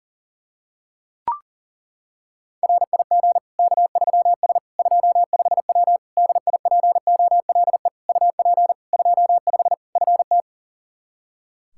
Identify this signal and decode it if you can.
Morse code, heard as a keyed tone, about 40 wpm, 700 Hz. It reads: RIG K3S 25W DIPOLE UP 25 FT
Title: K1USN SST QSO Practice 40wpm